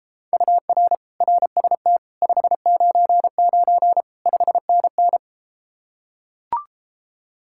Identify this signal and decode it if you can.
Morse code, heard as a keyed tone, about 33 wpm, 700 Hz. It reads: UR RST 599 5NN